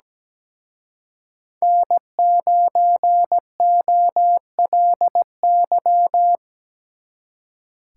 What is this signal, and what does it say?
Morse code, heard as a keyed tone, about 20 wpm, 700 Hz. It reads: N9OLY
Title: Speed Racing - USA Call Signs 17wpm